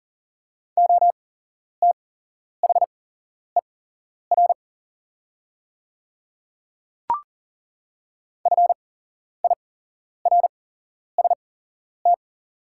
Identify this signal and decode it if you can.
Morse code, heard as a keyed tone, about 37 wpm, spaced as for 11 wpm, 700 Hz. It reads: OTHER FIRST
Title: 8x Character Spacing - Top 100 Words - 40wpm